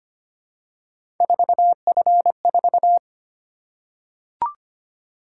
Morse code, heard as a keyed tone, about 25 wpm, 700 Hz. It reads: 4F4
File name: Three Letter-Number Combinations - 25wpm